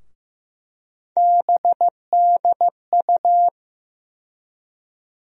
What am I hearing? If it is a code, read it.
Morse code, heard as a keyed tone, about 15 wpm, 700 Hz. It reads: BDU
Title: USA Call Sign Postfixes - 3 Letters 15wpm